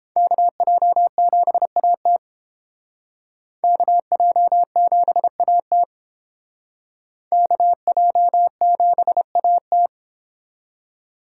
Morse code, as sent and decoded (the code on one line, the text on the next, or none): -.- .--- --... .- - / -.- .--- --... .- - / -.- .--- --... .- -
KJ7AT KJ7AT KJ7AT